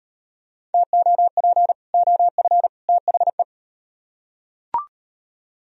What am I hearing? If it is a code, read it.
Morse code, heard as a keyed tone, about 37 wpm, 700 Hz. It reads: TOP OF THE